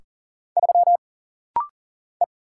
Morse code, heard as a keyed tone, about 40 wpm, 700 Hz. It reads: 3 E
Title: Single Letter-Number Alt - Rapid-Fire 40wpm